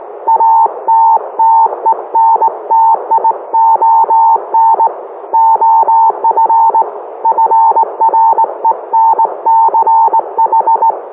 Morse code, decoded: ATTENTION OF FRENCH